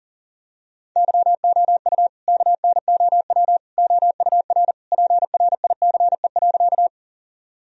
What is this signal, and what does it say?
Morse code, heard as a keyed tone, about 40 wpm, 700 Hz. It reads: YOU KNOW OUR PRICE.
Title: Sentences from Top 1000 Words - Review Part 4 - 40wpm